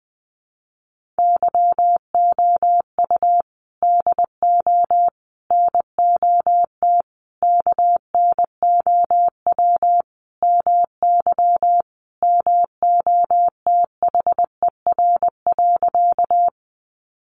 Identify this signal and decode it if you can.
Morse code, heard as a keyed tone, about 20 wpm, 700 Hz. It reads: YOU DO NOT KNOW MY MOTHER.
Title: Sentences from Top 600 Words 20wpm